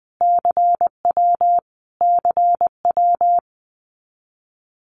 Morse code, heard as a keyed tone, about 20 wpm, 700 Hz. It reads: CW CW
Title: Abbreviations Repeated 3x - 20wpm